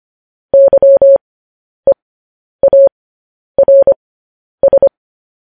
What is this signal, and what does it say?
Morse code, heard as a keyed tone, about 25 wpm, spaced as for 10 wpm, 550 Hz. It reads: YEARS
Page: news headlines in morse code at 10 wpm